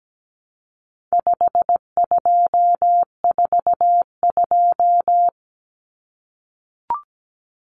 Morse code, heard as a keyed tone, about 17 wpm, 700 Hz. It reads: 5242
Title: Numbers - 4-Digit 17wpm